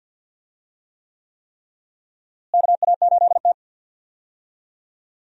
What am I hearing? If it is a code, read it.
Morse code, heard as a keyed tone, about 46 wpm, 700 Hz. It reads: KA8T
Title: USA Call Signs - 4 Characters - Repeated 3x - 50wpm